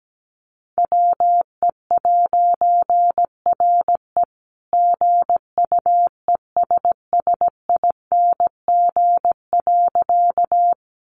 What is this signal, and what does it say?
Morse code, heard as a keyed tone, about 17 wpm, 700 Hz. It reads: WE'RE GUESSING.
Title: A Sound Of Thunder 17wpm